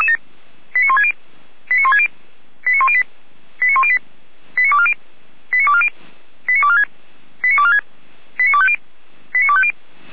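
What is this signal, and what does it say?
Five-tone selective call (ZVEI), heard as quick runs of five tones, tones of about 70 ms. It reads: 89160 89160 89108 89108 89240 89240 89246 89246 89250 89250
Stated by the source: ZVEI five-tone selective call 89336 received with an SDR.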